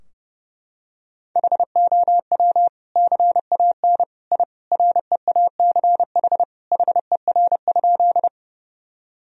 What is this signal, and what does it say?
Morse code, heard as a keyed tone, about 30 wpm, 700 Hz. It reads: HOW CAN I REACH HER?
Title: Sentences from Top 1000 Words - Review Part 2 - 30wpm